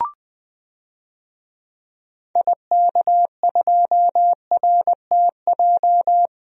Morse code, heard as a keyed tone, about 21 wpm, 700 Hz. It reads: IK2RTJ
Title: DXCC-100 Call Signs - 6 Characters - Repeated 2x - 20wpm